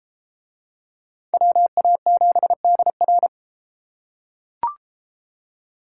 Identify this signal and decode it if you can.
Morse code, heard as a keyed tone, about 31 wpm, 700 Hz. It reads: WA7DR